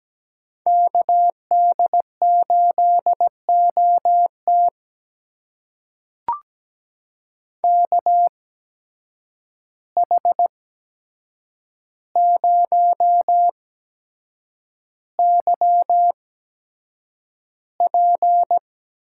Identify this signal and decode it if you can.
Morse code, heard as a keyed tone, about 17 wpm, spaced as for 7 wpm, 700 Hz. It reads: KD8OT KH0YP
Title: USA Call Signs - 5 Characters - 8x Character Spacing - 17wpm